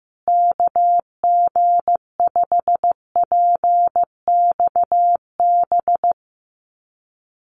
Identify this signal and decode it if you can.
Morse code, heard as a keyed tone, about 15 wpm, 700 Hz. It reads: KG5PXB
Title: USA Call Signs Repeated 2x 15wpm